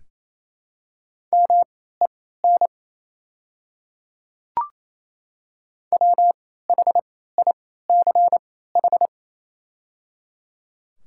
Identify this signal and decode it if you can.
Morse code, heard as a keyed tone, about 27 wpm, spaced as for 16 wpm, 700 Hz. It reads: MEN WHICH
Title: 3x Character Spacing - Top 200 Words - 28wpm